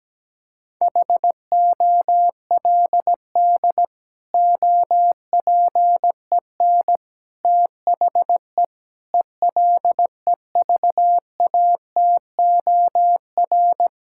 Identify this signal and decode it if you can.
Morse code, heard as a keyed tone, about 17 wpm, 700 Hz. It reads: HOLD OPEN THE ELEVATOR
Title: Sentences from Top 1600 Words 17wpm